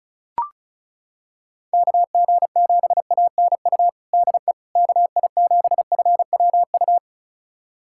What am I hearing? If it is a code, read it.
Morse code, heard as a keyed tone, about 36 wpm, 700 Hz. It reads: KG7ANU DE KI7FWU
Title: USA Callsign DE Callsign - 6 Characters - 35wpm